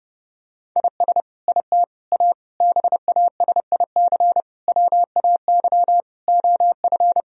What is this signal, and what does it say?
Morse code, heard as a keyed tone, about 30 wpm, 700 Hz. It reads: IS IT A BASIC WAY OF
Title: Sentences from Top 1000 Words - Review Part 3 - 30wpm